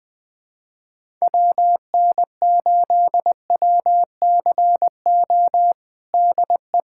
Morse code, heard as a keyed tone, about 20 wpm, 700 Hz. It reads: WN8WCO DE
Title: USA Callsign DE Callsign - 6 Characters - 20wpm